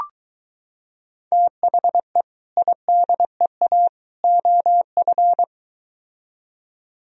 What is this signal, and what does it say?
Morse code, heard as a keyed tone, about 23 wpm, 700 Hz. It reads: THE IDEA OF